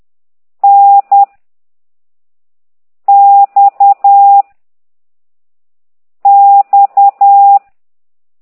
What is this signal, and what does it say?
Morse code, heard as a keyed tone, about 10 wpm, 800 Hz. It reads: N X X
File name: ct2idw / exerc9